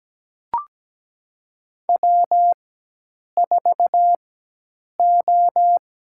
Morse code, heard as a keyed tone, about 17 wpm, spaced as for 8 wpm, 700 Hz. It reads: W4O
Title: USA Call Signs - 4x Character Spacing - 17wpm